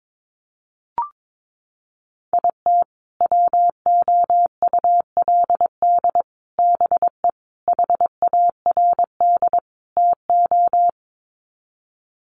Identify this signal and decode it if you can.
Morse code, heard as a keyed tone, about 22 wpm, 700 Hz. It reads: IT WOULD BE HARD TO
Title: Sets of 5 Words Encore2 22wpm